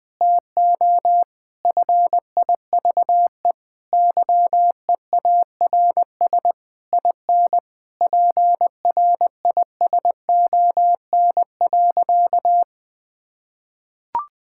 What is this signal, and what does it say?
Morse code, heard as a keyed tone, about 20 wpm, 700 Hz. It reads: TO FIVE YEARS IN PRISON.
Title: Sentences with 6-Word Phrases - 20wpm